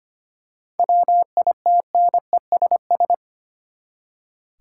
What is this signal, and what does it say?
Morse code, heard as a keyed tone, about 25 wpm, 700 Hz. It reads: WITNESS
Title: Seven Letter Words - 25wpm